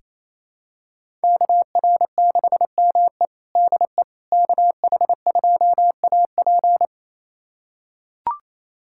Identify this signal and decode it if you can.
Morse code, heard as a keyed tone, about 27 wpm, 700 Hz. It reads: KR6ME DE KH2AP
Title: USA Callsign DE Callsign - 5 Characters - 28wpm